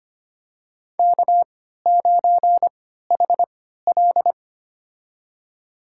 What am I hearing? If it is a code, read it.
Morse code, heard as a keyed tone, about 25 wpm, spaced as for 14 wpm, 700 Hz. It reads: K9HL
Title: USA Call Signs - 4 Characters - 3x Character Spacing - 25wpm